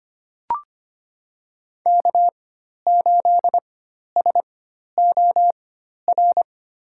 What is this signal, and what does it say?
Morse code, heard as a keyed tone, about 25 wpm, spaced as for 12 wpm, 700 Hz. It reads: K8SOR
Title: USA Call Signs - 5 Characters - 4x Character Spacing - 25wpm